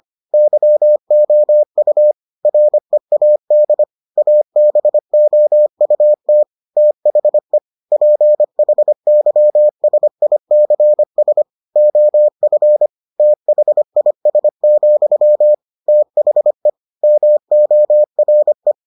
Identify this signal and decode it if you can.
Morse code, heard as a keyed tone, about 25 wpm, 600 Hz. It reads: YOU READ ABOUT THE PHYSICS OF THIS, THE MORE